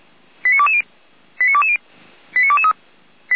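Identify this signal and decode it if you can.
Five-tone selective call (ZVEI), heard as quick runs of five tones, tones of about 70 ms. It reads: 89220 89220 89223
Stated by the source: ZVEI selective call 89010 received on an SDR by the Orlamünde fire brigade.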